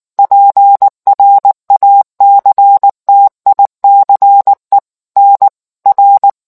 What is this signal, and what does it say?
Morse code, heard as a keyed tone, about 19 wpm, 800 Hz. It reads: PRACTICE NR